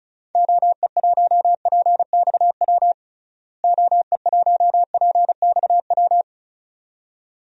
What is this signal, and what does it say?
Morse code, heard as a keyed tone, about 34 wpm, 700 Hz. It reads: OE1PXW OE1PXW
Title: DXCC-100 Call Signs - 6 Characters - Repeated 2x - 35wpm